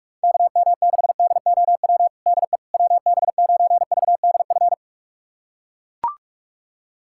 Morse code, heard as a keyed tone, about 41 wpm, 700 Hz. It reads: KM6DOW DE WB9VDF
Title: USA Callsign DE Callsign - 6 Characters - 45wpm